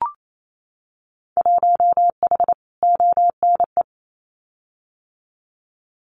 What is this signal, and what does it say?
Morse code, heard as a keyed tone, about 28 wpm, 700 Hz. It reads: 1H ONE
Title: Winter Field Day Exchanges Encore 28wpm